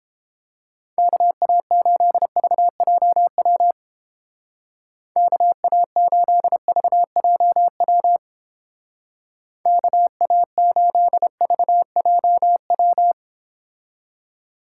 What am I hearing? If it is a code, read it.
Morse code, heard as a keyed tone, about 29 wpm, 700 Hz. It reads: KA8VJW KA8VJW KA8VJW